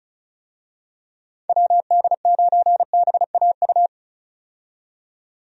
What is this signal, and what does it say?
Morse code, heard as a keyed tone, about 36 wpm, 700 Hz. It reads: WD9BAU
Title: USA Call Signs - 6 Characters - 35wpm